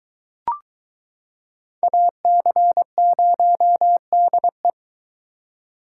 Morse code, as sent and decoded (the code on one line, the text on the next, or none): .- -.-. ----- -.. .
AC0DE